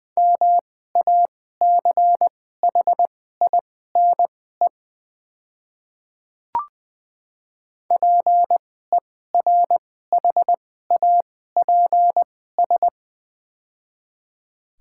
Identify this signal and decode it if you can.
Morse code, heard as a keyed tone, about 21 wpm, spaced as for 15 wpm, 700 Hz. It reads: MACHINE PERHAPS
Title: 2x Character Spacing - Top 500 Words - 20wpm